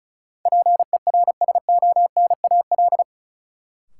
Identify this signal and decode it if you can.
Morse code, heard as a keyed tone, about 34 wpm, 700 Hz. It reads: PERSONAL
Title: Sentences from Top 700 Words - New Word Drill - 35wpm